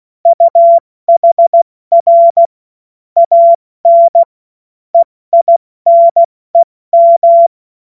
Morse code, heard as a keyed tone, about 16 wpm, 680 Hz. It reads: UHR AN EINEM
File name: Morsekurs / Witze / Witz 23. Dezember 2025 70 F80